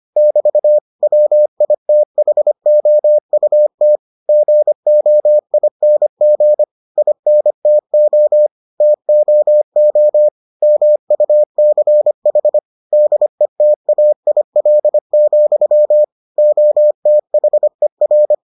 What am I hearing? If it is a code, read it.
Morse code, heard as a keyed tone, about 25 wpm, 600 Hz. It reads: <BT> WITHOUT GOING INTO TOO MUCH DETAIL, OTHER